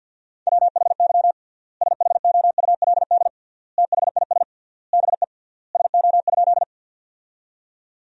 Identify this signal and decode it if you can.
Morse code, heard as a keyed tone, about 53 wpm, 700 Hz. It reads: WHY SHOULD THIS BE SO?
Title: 3x Word-Spacing - Sentences from Top 300 Words - 50wpm